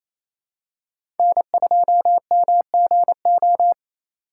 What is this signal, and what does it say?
Morse code, heard as a keyed tone, about 28 wpm, 700 Hz. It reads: N2MGO
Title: USA Call Signs - 5 Characters - 2x Character Spacing - 28wpm